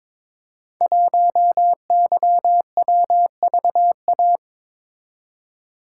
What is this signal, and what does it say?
Morse code, heard as a keyed tone, about 22 wpm, 700 Hz. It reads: 1YWVA
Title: Code Groups - 5 Letter:Numbers - 22wpm